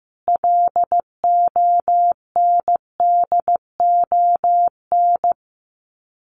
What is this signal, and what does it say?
Morse code, heard as a keyed tone, about 15 wpm, 700 Hz. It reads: LONDON
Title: Common QSO Phrases 15wpm